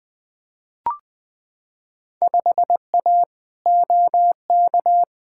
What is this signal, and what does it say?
Morse code, heard as a keyed tone, about 20 wpm, 700 Hz. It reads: 5A OK
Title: Field Day Exchanges - 20wpm